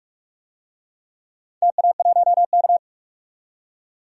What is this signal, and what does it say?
Morse code, heard as a keyed tone, about 44 wpm, 700 Hz. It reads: TA1K
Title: DXCC-100 Call Signs - 4 Characters - 45wpm